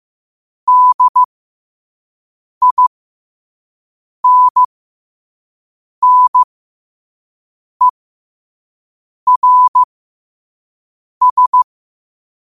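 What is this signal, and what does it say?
Morse code, heard as a keyed tone, about 15 wpm, spaced as for 5 wpm, 1000 Hz. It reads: DINNERS
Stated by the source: Morse quote-of-the-day practice at 5 wpm.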